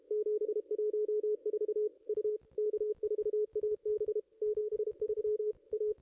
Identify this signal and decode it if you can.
Morse code, heard as a keyed tone, about 29 wpm, 440 Hz, with radio noise. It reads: 714 U K4AB 73 A
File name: contests / ss-cw-k5tr-op-n5rz -0600